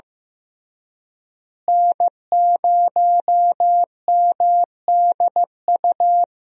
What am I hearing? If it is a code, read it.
Morse code, heard as a keyed tone, about 15 wpm, 700 Hz. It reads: N0MDU
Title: USA Call Signs - 5 Characters - Repeated 3x - 15wpm